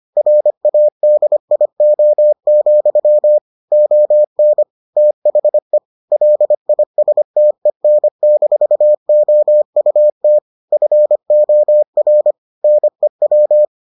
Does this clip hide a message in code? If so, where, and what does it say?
Yes: Morse code from 0.2 to 13.6 s, heard as a keyed tone, about 25 wpm, 600 Hz. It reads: RADIO, ON THE LISTEN-OUT FOR NEW